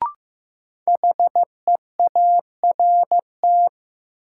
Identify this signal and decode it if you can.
Morse code, heard as a keyed tone, about 15 wpm, 700 Hz. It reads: HEART
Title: Sentences from Top 1200 Words - New Word Drill Alt - 15wpm